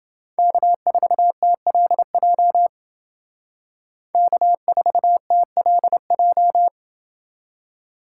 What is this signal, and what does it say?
Morse code, heard as a keyed tone, about 28 wpm, 700 Hz. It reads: K4TLJ K4TLJ